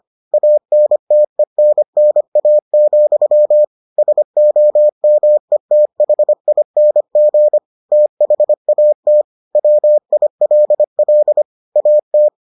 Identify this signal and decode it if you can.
Morse code, heard as a keyed tone, about 25 wpm, 600 Hz. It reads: ANTENNA, SOMETHING THAT WILL AT